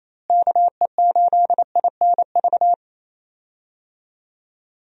Morse code, heard as a keyed tone, about 28 wpm, 700 Hz. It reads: KE8INV
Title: USA Call Signs - 6 Characters - Repeated 3x - 28wpm